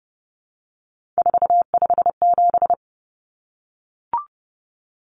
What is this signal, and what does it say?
Morse code, heard as a keyed tone, about 30 wpm, 700 Hz. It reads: 457
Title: Common Cut Numbers - 3-Digit 30wpm